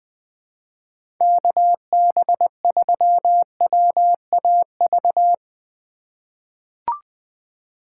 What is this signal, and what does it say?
Morse code, heard as a keyed tone, about 20 wpm, 700 Hz. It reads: KB3WAV
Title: 2023 POTA Activator Call Signs - 20wpm